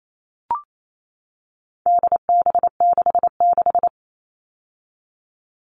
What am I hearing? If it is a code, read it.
Morse code, heard as a keyed tone, about 28 wpm, 700 Hz. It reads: DB66
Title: All Cut Numbers - 4-Digit 28wpm